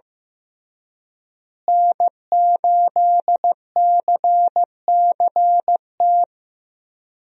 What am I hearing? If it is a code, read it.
Morse code, heard as a keyed tone, about 16 wpm, 700 Hz. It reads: N8CCT
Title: USA Call Signs - 5 Characters - Repeated 3x - 15wpm